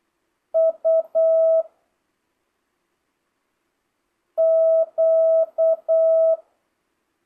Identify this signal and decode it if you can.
Morse code, heard as a keyed tone, about 8 wpm, 650 Hz. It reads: U Q
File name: jh8cbh / 2025cw